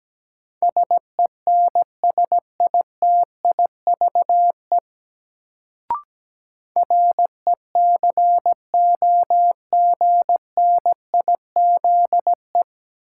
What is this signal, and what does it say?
Morse code, heard as a keyed tone, about 17 wpm, 700 Hz. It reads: SENSITIVE RECOGNIZE
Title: Sentences from Top 1400 Words - New Word Drill Alt - 17wpm